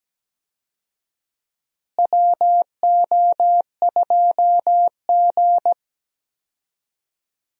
Morse code, heard as a keyed tone, about 17 wpm, 700 Hz. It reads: WO2G
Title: USA Call Signs - 4 Characters - Repeated 2x - 17wpm